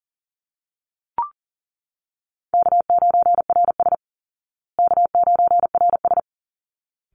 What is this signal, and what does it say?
Morse code, heard as a keyed tone, about 40 wpm, 700 Hz. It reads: K9RS K9RS
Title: Easy Call Signs Repeated 2x - 40wpm